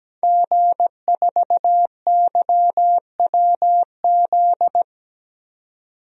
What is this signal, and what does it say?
Morse code, heard as a keyed tone, about 17 wpm, 700 Hz. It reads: G4YWZ
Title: DXCC-100 Call Signs - 5 Characters - 17wpm